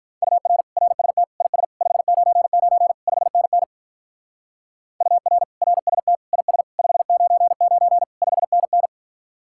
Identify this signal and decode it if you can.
Morse code, heard as a keyed tone, about 42 wpm, 700 Hz. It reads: UR RST IS 599 5NN UR RST IS 599 5NN
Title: Speed-Racing Common QSO Phrases - 35wpm